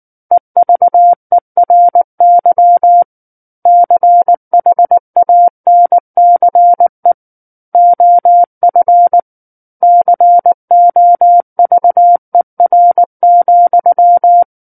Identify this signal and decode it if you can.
Morse code, heard as a keyed tone, about 19 wpm, 700 Hz. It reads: EVERY CHANCE OF COVER,
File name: Book One / 10-Chapter 10 - 19 WPM